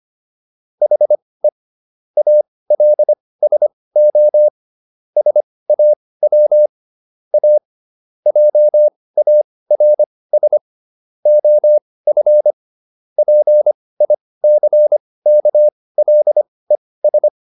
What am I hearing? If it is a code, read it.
Morse code, heard as a keyed tone, about 25 wpm, spaced as for 18 wpm, 600 Hz. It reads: HE ALSO SAW A JARS OF PICKLES